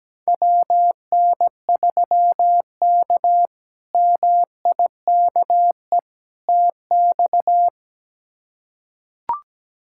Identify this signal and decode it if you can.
Morse code, heard as a keyed tone, about 17 wpm, 700 Hz. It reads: WN3K MIKE TX
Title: K1USN SST QSO Practice - 17wpm